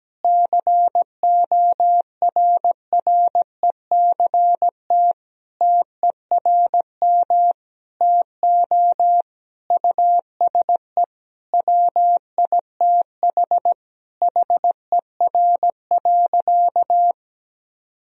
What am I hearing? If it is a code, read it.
Morse code, heard as a keyed tone, about 17 wpm, 700 Hz. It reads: CORRECT TERM TO USE WITH HER.
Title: Sentences from Top 1000 Words - Review Part 4 - 17wpm